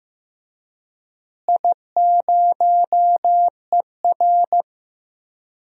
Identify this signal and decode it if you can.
Morse code, heard as a keyed tone, about 15 wpm, 700 Hz. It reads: I0ER
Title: DXCC-100 Call Signs - 4 Characters - Repeated 2x - 15wpm